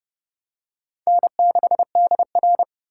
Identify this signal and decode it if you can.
Morse code, heard as a keyed tone, about 29 wpm, 700 Hz. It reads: N6DR